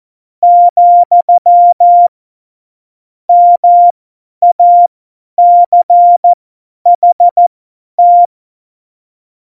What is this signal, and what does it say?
Morse code, heard as a keyed tone, about 14 wpm, spaced as for 10 wpm, 700 Hz. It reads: , MACHT
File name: Morsekurs / Witze / Witz 12 Mai 2020, 50 F70